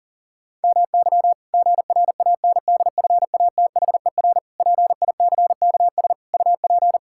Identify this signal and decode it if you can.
Morse code, heard as a keyed tone, about 40 wpm, 700 Hz. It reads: MY GRANDFATHER PICKS UP
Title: Pangram Challenge v2 - 40wpm